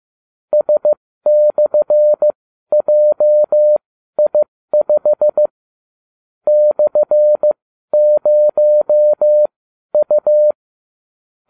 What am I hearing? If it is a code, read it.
Morse code, heard as a keyed tone, about 15 wpm, spaced as for 12 wpm, 600 Hz. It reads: S/JI5 /0U